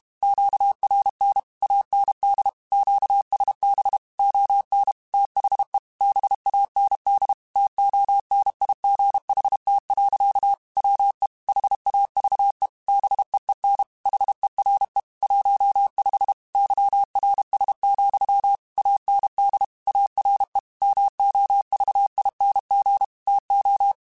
Morse code, heard as a keyed tone, about 32 wpm, 800 Hz. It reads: QRN AND QSB ON THE BAND TONIGHT. WE HAVE BEEN HERE 15 YRS, AND ARE MOVING TO
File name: QSOs / QSOs-mp3s / WD0BGP 1@30